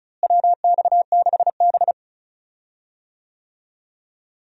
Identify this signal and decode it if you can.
Morse code, heard as a keyed tone, about 36 wpm, 700 Hz. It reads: WX6B
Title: USA Call Signs - 4 Characters - Repeated 2x - 35wpm